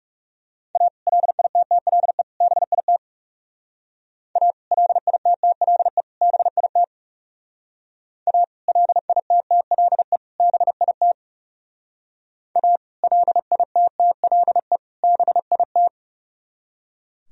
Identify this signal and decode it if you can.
Morse code, heard as a keyed tone, about 38 wpm, 700 Hz. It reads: A LITTLE BIT A LITTLE BIT A LITTLE BIT A LITTLE BIT